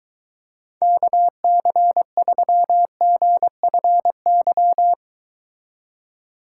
Morse code, heard as a keyed tone, about 23 wpm, 700 Hz. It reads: KC3GFY